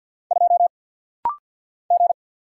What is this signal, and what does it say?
Morse code, heard as a keyed tone, about 50 wpm, 700 Hz. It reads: 2 G
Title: Single Letter-Number Alt - Rapid-Fire 50wpm